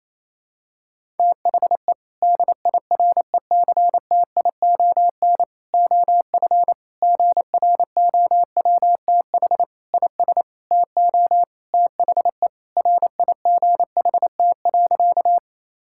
Morse code, heard as a keyed tone, about 28 wpm, 700 Hz. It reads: THE DIRECTION OF GROWTH IS TO THE RIGHT.
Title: Sentences from Top 1000 Words - Review Part 5 - 28wpm